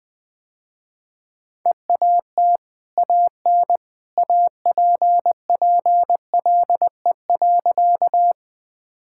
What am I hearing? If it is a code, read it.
Morse code, heard as a keyed tone, about 20 wpm, 700 Hz. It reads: EAT AN APPLE.
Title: Easy Sentences with Hints - 20wpm